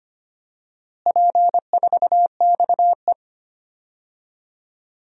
Morse code, heard as a keyed tone, about 25 wpm, 700 Hz. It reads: P4XE
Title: DXCC-100 Call Signs - 4 Characters - Repeated 3x - 25wpm